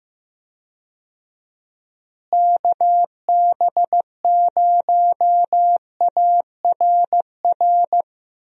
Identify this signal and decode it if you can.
Morse code, heard as a keyed tone, about 15 wpm, 700 Hz. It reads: KB0ARR
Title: USA Call Signs - 6 Characters - Repeated 3x - 15wpm